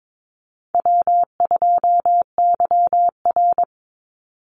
Morse code, heard as a keyed tone, about 22 wpm, 700 Hz. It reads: W2YR
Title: Easy Call Signs Repeated 3x - 22wpm